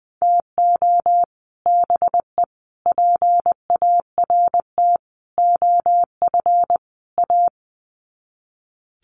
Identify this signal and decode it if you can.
Morse code, heard as a keyed tone, about 20 wpm, 700 Hz. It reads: TO BE PART OF A
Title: Sets of 5 Words Encore2 20wpm